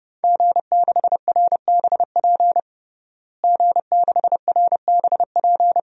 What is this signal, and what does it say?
Morse code, heard as a keyed tone, about 30 wpm, 700 Hz. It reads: G6RBP G6RBP
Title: DXCC-100 Call Signs - 5 Characters - Repeated 2x - 30wpm